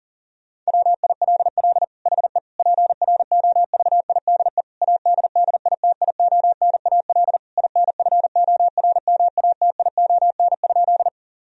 Morse code, heard as a keyed tone, about 37 wpm, 700 Hz. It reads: WILL HE PROVIDE ADDITIONAL INFORMATION?
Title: Sentences from Top 1000 Words - Review Part 4 - 40wpm